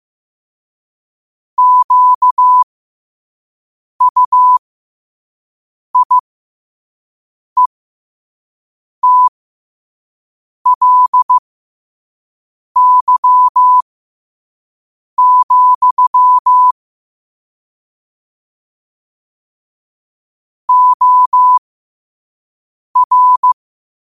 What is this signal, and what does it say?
Morse code, heard as a keyed tone, about 15 wpm, spaced as for 5 wpm, 1000 Hz. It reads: QUIETLY, OR
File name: qotdfiles / 5wpm